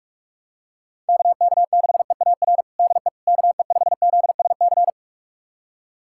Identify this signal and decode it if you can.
Morse code, heard as a keyed tone, about 46 wpm, 690 Hz. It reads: KK6EAR DE KE5ZSC
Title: USA Callsign DE Callsign 45wpm